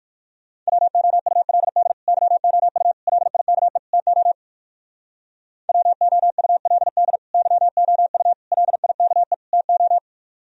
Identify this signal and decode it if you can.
Morse code, heard as a keyed tone, about 58 wpm, 700 Hz. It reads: WOULD YOU LIKE TO WOULD YOU LIKE TO